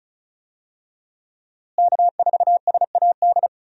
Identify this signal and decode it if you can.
Morse code, heard as a keyed tone, about 36 wpm, 700 Hz. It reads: K4SAD
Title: USA Call Signs - 5 Characters - Repeated 3x - 35wpm